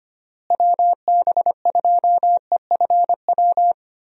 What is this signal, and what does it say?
Morse code, heard as a keyed tone, about 25 wpm, 700 Hz. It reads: WB2EFW
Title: USA Call Signs - 6 Characters - Repeated 3x - 25wpm